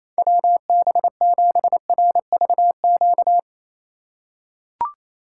Morse code, heard as a keyed tone, about 28 wpm, 700 Hz. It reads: WB7RVQ
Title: USA Call Signs - 4x Character Spacing - 28wpm